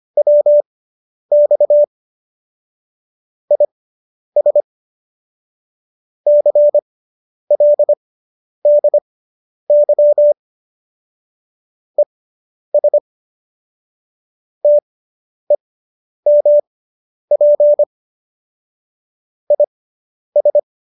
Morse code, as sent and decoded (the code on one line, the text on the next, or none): .-- -..- / .. ... / -.-. .-.. -.. -.-- / . ... / - . -- .--. / .. ...
WX IS CLDY ES TEMP IS